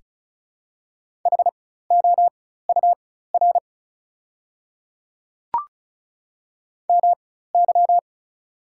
Morse code, heard as a keyed tone, about 36 wpm, spaced as for 16 wpm, 700 Hz. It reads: HOUR MY
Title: 4x Character Spacing - Top 500 Words - 35wpm